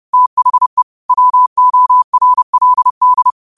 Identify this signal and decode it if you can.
Morse code, heard as a keyed tone, about 30 wpm, 1000 Hz. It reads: THE WORLD